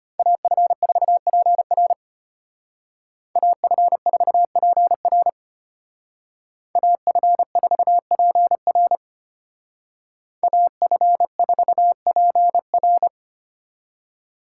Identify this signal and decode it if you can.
Morse code, heard as a keyed tone, about 33 wpm, 700 Hz. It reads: AF4PR AF4PR AF4PR AF4PR